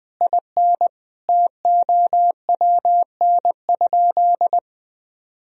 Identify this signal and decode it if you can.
Morse code, heard as a keyed tone, about 20 wpm, 700 Hz. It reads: IN TOWN?
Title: Sentences from Top 500 Words and Abbreviations 20wpm